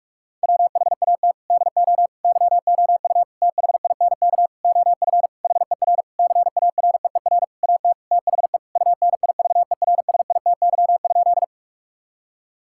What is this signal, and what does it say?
Morse code, heard as a keyed tone, about 46 wpm, 700 Hz. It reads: WHAT DO YOU THINK OF HER CAREER AT THE UNIVERSITY?
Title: Sentences from Top 1000 Words - Review Part 6 - 45wpm